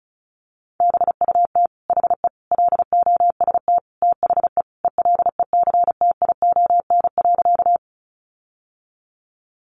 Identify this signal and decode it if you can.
Morse code, heard as a keyed tone, about 34 wpm, 700 Hz. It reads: BUT HE LOST THE ELECTION.
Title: Sentences from Top 1500 Words 35wpm